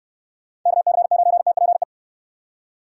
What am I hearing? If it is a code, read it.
Morse code, heard as a keyed tone, about 69 wpm, 700 Hz. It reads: KC9TPE